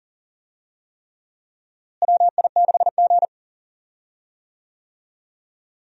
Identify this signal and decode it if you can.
Morse code, heard as a keyed tone, about 39 wpm, 700 Hz. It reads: WI6G
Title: USA Call Signs - 4 Characters - Repeated 3x - 40wpm